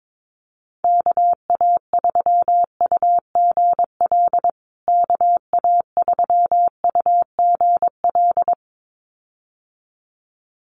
Morse code, heard as a keyed tone, about 22 wpm, 700 Hz. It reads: KA3UGL KA3UGL
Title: USA Call Signs Repeated 2x 22wpm